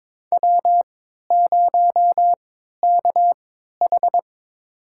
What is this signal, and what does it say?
Morse code, heard as a keyed tone, about 21 wpm, spaced as for 12 wpm, 700 Hz. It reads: W0KH
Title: USA Call Signs - 4 Characters - 3x Character Spacing - 22wpm